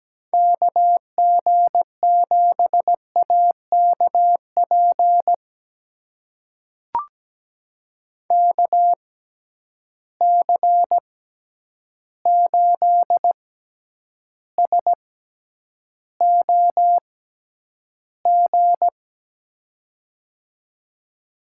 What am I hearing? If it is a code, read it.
Morse code, heard as a keyed tone, about 17 wpm, spaced as for 9 wpm, 700 Hz. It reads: KG7AKP KC8SOG
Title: USA Call Signs - 6 Characters - 6x Character Spacing - 17wpm